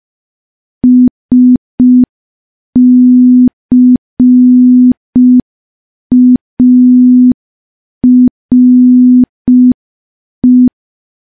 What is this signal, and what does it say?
Morse code, heard as a keyed tone, about 5 wpm, 250 Hz. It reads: SCARE
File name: morse / morse-training-05WPM